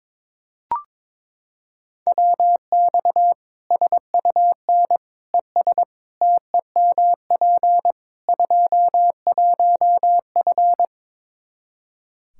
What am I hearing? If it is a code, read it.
Morse code, heard as a keyed tone, about 22 wpm, 700 Hz. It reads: WX SUN ES TEMP 21F
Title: Practice QSO - Exchange 2 - Info - 22wpm